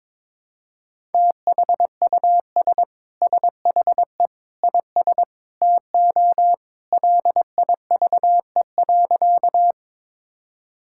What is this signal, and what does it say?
Morse code, heard as a keyed tone, about 22 wpm, 700 Hz. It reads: THUS SHE IS TO LIVE.
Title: Sentences from Top 1000 Words - Review Part 5 - 22wpm